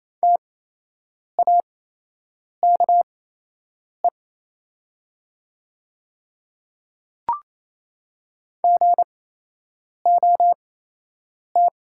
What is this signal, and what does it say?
Morse code, heard as a keyed tone, about 28 wpm, spaced as for 8 wpm, 700 Hz. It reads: TAKE GOT
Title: 8x Character Spacing - Top 300 Words - 28wpm